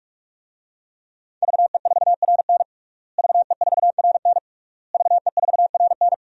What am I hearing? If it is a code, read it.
Morse code, heard as a keyed tone, about 46 wpm, 700 Hz. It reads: VE4RN VE4RN VE4RN
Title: DXCC-100 Call Signs - 5 Characters - Repeated 3x - 45wpm